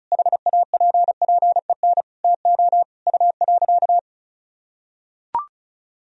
Morse code, heard as a keyed tone, about 36 wpm, 700 Hz. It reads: HAPPEN TO U.
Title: Sentences from Top 500 Words and Abbreviations 35wpm